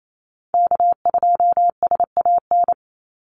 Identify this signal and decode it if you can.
Morse code, heard as a keyed tone, about 28 wpm, 700 Hz. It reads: K2SAN
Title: Call Signs - 28wpm